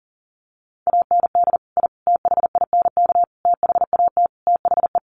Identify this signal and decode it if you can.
Morse code, heard as a keyed tone, about 40 wpm, 700 Hz. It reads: AND I THINK THAT THE
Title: Sets of 5 Words 40wpm